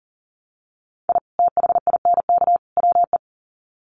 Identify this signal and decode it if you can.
Morse code, heard as a keyed tone, about 40 wpm, 700 Hz. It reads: I THINK WE
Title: Sets of 3 Words 40wpm